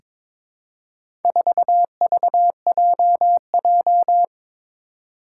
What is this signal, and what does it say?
Morse code, heard as a keyed tone, about 23 wpm, 700 Hz. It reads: 4VJJ
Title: Four Challenging Letter-Number Combinations 22wpm